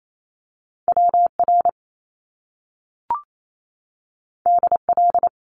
Morse code, heard as a keyed tone, about 28 wpm, 700 Hz. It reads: WR DL
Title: Top Two Letter Combinations in Words 28wpm